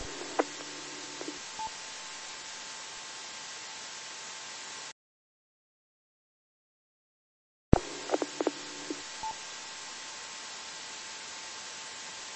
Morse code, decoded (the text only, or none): none